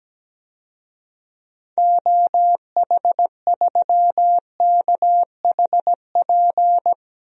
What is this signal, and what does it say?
Morse code, heard as a keyed tone, about 17 wpm, 700 Hz. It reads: OH3KHP
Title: DXCC-100 Call Signs - 6 Characters - 17wpm